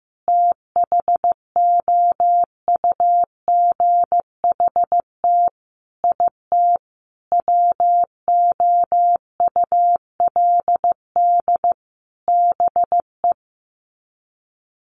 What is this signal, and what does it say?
Morse code, heard as a keyed tone, about 15 wpm, 700 Hz. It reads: THOUGHT IT WOULD BE
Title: Sets of 5 Words Encore 15wpm